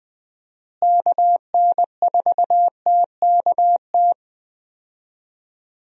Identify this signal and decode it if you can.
Morse code, heard as a keyed tone, about 20 wpm, 700 Hz. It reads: KN4TKT
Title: USA Call Signs - 6 Characters - Repeated 3x - 20wpm